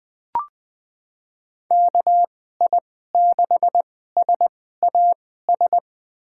Morse code, heard as a keyed tone, about 20 wpm, spaced as for 14 wpm, 700 Hz. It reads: KI6SAS
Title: USA Call Signs - 6 Characters - 2x Character Spacing - 20wpm